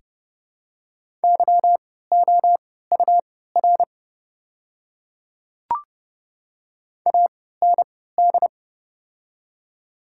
Morse code, heard as a keyed tone, about 30 wpm, spaced as for 17 wpm, 700 Hz. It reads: YOUR AND
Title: 3x Character Spacing - Top 100 Words - 30wpm